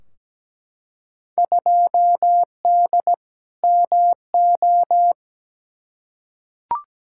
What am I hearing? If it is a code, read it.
Morse code, heard as a keyed tone, about 17 wpm, 700 Hz. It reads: 2D MO
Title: Field Day Exchanges - 17wpm